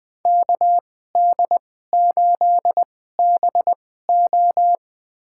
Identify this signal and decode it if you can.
Morse code, heard as a keyed tone, about 21 wpm, spaced as for 15 wpm, 700 Hz. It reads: KD8BO
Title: USA Call Signs - 5 Characters - 2x Character Spacing - 20wpm